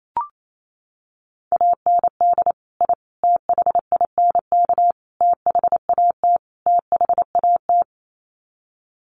Morse code, heard as a keyed tone, about 28 wpm, 700 Hz. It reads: AND I THINK THAT THAT
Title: Sets of 5 Words Encore 28wpm